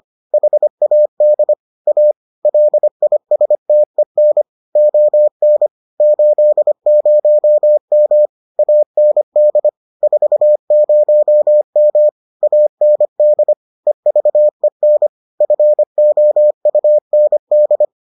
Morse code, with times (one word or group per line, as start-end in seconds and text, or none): 0.3-1.5 HAD
1.9-2.1 A
2.4-4.4 LISTEN
4.8-5.7 ON
6.0-8.3 80M
8.6-9.7 AND
10.0-12.1 40M
12.4-13.5 AND
13.9-15.1 EVEN
15.4-17.9 FOUND